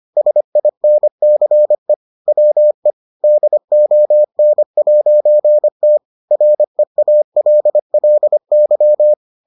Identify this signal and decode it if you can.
Morse code, heard as a keyed tone, about 25 wpm, 600 Hz. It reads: SINCE WE DON'T REALLY